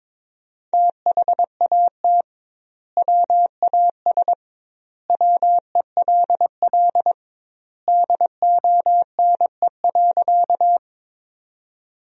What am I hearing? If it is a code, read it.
Morse code, heard as a keyed tone, about 22 wpm, 700 Hz. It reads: THAT WAS WELL DONE.
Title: 2x Word-Spacing - Sentences from Top 500 Words - 22wpm